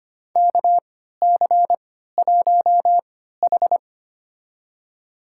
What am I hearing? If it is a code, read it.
Morse code, heard as a keyed tone, about 25 wpm, spaced as for 14 wpm, 700 Hz. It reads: KC1H
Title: USA Call Signs - 3x Character Spacing - 25wpm